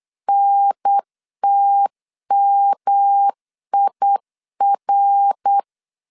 Morse code, heard as a keyed tone, about 8 wpm, 800 Hz. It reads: NTMIR
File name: CW / DET / 1 3 1500C S B 8wpm 4141